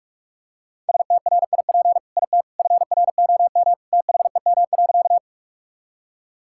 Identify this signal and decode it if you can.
Morse code, heard as a keyed tone, about 46 wpm, 700 Hz. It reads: STRIP IT FROM THEM.
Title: Sentences from Top 1600 Words 45wpm